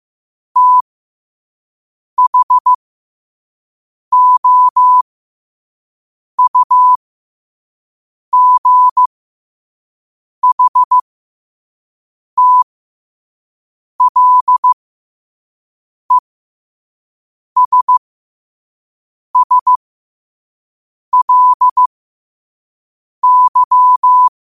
Morse code, heard as a keyed tone, about 15 wpm, spaced as for 5 wpm, 1000 Hz. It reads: THOUGHTLESSLY
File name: qotdfiles / 5wpm